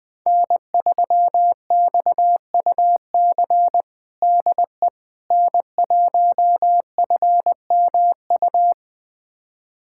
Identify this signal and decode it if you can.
Morse code, heard as a keyed tone, about 20 wpm, 700 Hz. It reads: N3XUC DE N1FMU
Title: USA Callsign DE Callsign - 5 Characters - 20wpm